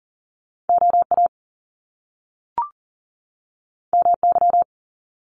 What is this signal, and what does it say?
Morse code, heard as a keyed tone, about 40 wpm, 700 Hz. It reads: OA MY
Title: Top Two Letter Combinations in Words 40wpm